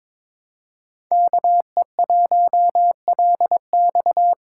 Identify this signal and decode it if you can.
Morse code, heard as a keyed tone, about 23 wpm, 700 Hz. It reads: KE1LX
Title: USA Call Signs - 5 Characters - 2x Character Spacing - 22wpm